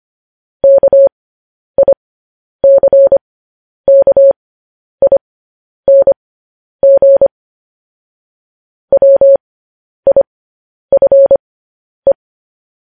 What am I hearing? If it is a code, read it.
Morse code, heard as a keyed tone, about 25 wpm, spaced as for 10 wpm, 550 Hz. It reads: KICKING WIFE
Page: news-headlines-in-morse-code-at-10-wpm